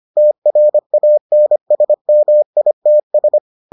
Morse code, heard as a keyed tone, about 25 wpm, 600 Hz. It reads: TRANSMITS